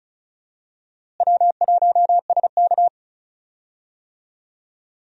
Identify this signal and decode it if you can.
Morse code, heard as a keyed tone, about 36 wpm, 700 Hz. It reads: W1SK
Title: USA Call Signs - 4 Characters - Repeated 3x - 35wpm